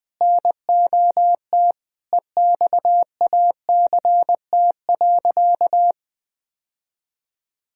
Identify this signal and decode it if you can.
Morse code, heard as a keyed tone, about 20 wpm, 700 Hz. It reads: NOT EXACT.
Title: Sentences from Top 1600 Words 20wpm